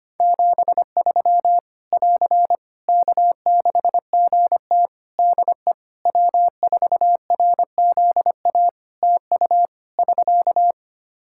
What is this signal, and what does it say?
Morse code, heard as a keyed tone, about 25 wpm, 700 Hz. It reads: 73 <AR> K6GT DE W4RZA TU <SK>
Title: Practice QSO - Exchange 4 - Ending - 25wpm